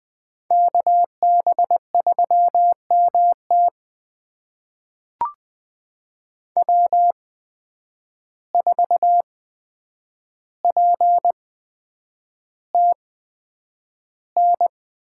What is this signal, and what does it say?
Morse code, heard as a keyed tone, about 20 wpm, spaced as for 9 wpm, 700 Hz. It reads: KB3MT W4PTN
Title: USA Call Signs - 5 Characters - 8x Character Spacing - 20wpm